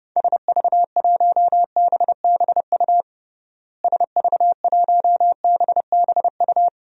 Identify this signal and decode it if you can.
Morse code, heard as a keyed tone, about 32 wpm, 700 Hz. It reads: SV1BBU SV1BBU
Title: DXCC-100 Call Signs - 6 Characters - Repeated 2x - 30wpm